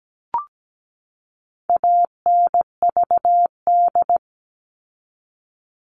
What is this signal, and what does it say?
Morse code, heard as a keyed tone, about 17 wpm, 700 Hz. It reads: ANVD
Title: All Cut Numbers - 4-Digit 17wpm